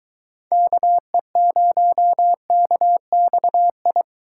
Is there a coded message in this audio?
Yes: Morse code from 0.5 to 4.0 s, heard as a keyed tone, about 22 wpm, 700 Hz. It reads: KE0KXI